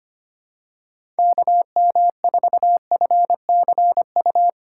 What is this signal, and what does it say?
Morse code, heard as a keyed tone, about 25 wpm, 700 Hz. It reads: KM4FCU
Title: USA Call Signs - 6 Characters - 8x Character Spacing - 25wpm